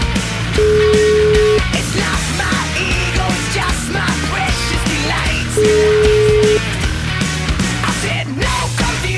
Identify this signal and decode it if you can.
Ringback tone (Europe, 425 Hz), with noise.